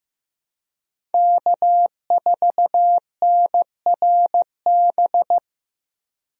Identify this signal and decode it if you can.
Morse code, heard as a keyed tone, about 15 wpm, 700 Hz. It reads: K4NRB
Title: USA Call Signs - 5 Characters - Repeated 3x - 15wpm